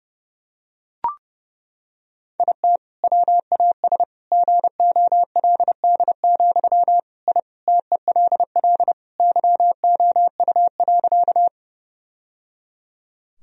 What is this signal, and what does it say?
Morse code, heard as a keyed tone, about 30 wpm, 700 Hz. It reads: IT WAS GOLD, I TELL YOU.
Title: Sentences from Top 1000 Words - Review Part 3 - 30wpm